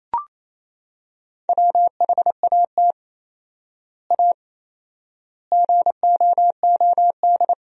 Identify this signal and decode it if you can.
Morse code, heard as a keyed tone, about 28 wpm, 700 Hz. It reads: WHAT A GOOD
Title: 4x Word-Spacing - Sentences from Top 200 Words - 28wpm